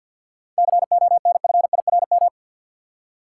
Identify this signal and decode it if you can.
Morse code, heard as a keyed tone, about 52 wpm, 700 Hz. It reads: CONFIRM
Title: Seven Letter Words - 50wpm